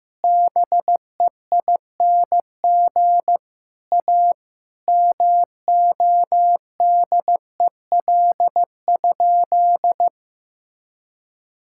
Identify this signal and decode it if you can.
Morse code, heard as a keyed tone, about 15 wpm, 700 Hz. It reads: BEING A MODEL?
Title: Sentences from Top 1000 Words - Review Part 5 - 15wpm